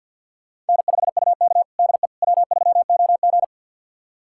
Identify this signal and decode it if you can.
Morse code, heard as a keyed tone, about 52 wpm, 700 Hz. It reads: N5UK DE W3OG
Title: USA Callsign DE Callsign - 4 Characters - 50wpm